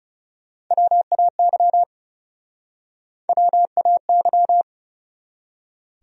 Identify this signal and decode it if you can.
Morse code, heard as a keyed tone, about 37 wpm, 700 Hz. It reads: WAY WAY
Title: Speed-Racing Top 100 Words 30wpm